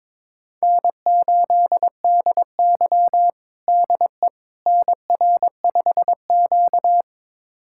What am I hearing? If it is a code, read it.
Morse code, heard as a keyed tone, about 22 wpm, 700 Hz. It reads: N8DY DE NR5Q
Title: USA Callsign DE Callsign - 4 Characters - 22wpm